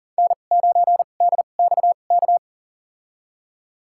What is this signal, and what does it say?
Morse code, heard as a keyed tone, about 37 wpm, spaced as for 28 wpm, 700 Hz. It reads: N9DXK
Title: USA Call Signs - 5 Characters - 2x Character Spacing - 40wpm